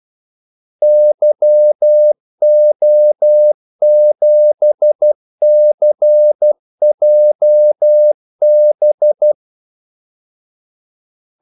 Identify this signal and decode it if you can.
Morse code, heard as a keyed tone, about 12 wpm, 600 Hz. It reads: YO7CJB
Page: 2 ekstra ordmellemrum mellem kaldesignalerne